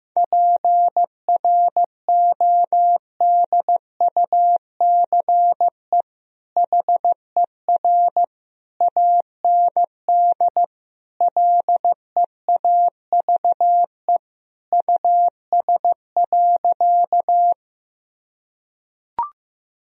Morse code, heard as a keyed tone, about 15 wpm, 700 Hz. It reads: PRODUCE HER AND LEAVE US.
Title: Sentences from Top 1000 Words - Review Part 3 - 15wpm